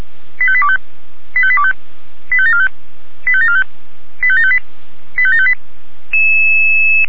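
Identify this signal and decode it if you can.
Five-tone selective call (ZVEI), heard as quick runs of five tones, tones of about 70 ms. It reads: 85825 85825 85745 85745 85758 85758
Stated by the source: ZVEI five-tone alert received by SDR.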